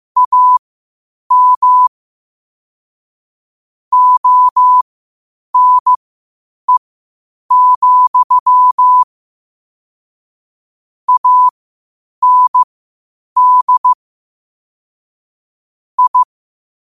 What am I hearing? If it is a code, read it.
Morse code, heard as a keyed tone, about 15 wpm, spaced as for 8 wpm, 1000 Hz. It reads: AM ONE, AND I
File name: qotdfiles / 8wpm